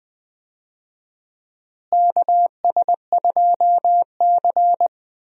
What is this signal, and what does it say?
Morse code, heard as a keyed tone, about 20 wpm, 700 Hz. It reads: KS2C
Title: USA Call Signs - 4 Characters - Repeated 2x - 20wpm